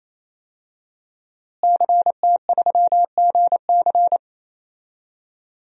Morse code, heard as a keyed tone, about 28 wpm, 700 Hz. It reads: CT3GC
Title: DXCC-100 Call Signs - 5 Characters - Repeated 3x - 28wpm